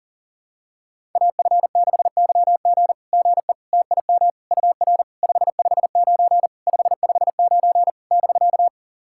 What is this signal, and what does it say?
Morse code, heard as a keyed tone, about 41 wpm, 690 Hz. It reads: AF6YG GE TIM UR 559 559 <BK>
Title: SOTA QSO Practice 40wpm